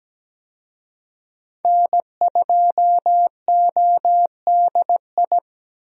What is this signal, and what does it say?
Morse code, heard as a keyed tone, about 17 wpm, 700 Hz. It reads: N2ODI
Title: USA Call Signs - 5 Characters - 17wpm